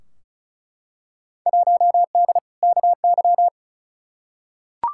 Morse code, heard as a keyed tone, about 34 wpm, 700 Hz. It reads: 1D KY
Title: Field Day Exchanges - 35wpm